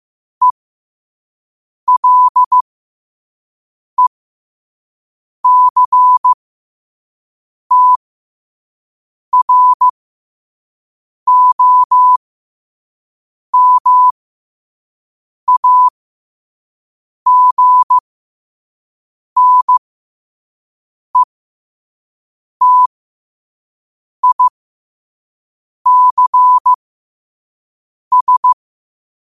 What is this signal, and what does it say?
Morse code, heard as a keyed tone, about 15 wpm, spaced as for 5 wpm, 1000 Hz. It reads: ELECTROMAGNETICS
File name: qotdfiles / 5wpm